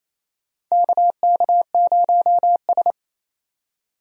Morse code, heard as a keyed tone, about 28 wpm, 700 Hz. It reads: KK0S
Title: USA Call Signs - 4 Characters - 28wpm